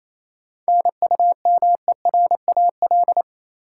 Morse code, heard as a keyed tone, about 28 wpm, 700 Hz. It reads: NUMERAL
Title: Sentences from Top 1000 Words - Review Part 2 - New Word Drill - 28wpm